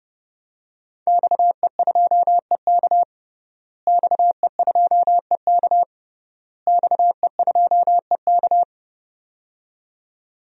Morse code, heard as a keyed tone, about 32 wpm, 700 Hz. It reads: XE2EK XE2EK XE2EK
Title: DXCC-100 Call Signs - 5 Characters - Repeated 3x - 30wpm